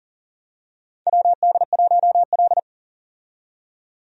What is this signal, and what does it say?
Morse code, heard as a keyed tone, about 38 wpm, 700 Hz. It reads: WD1L
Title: USA Call Signs - 4 Characters - 40wpm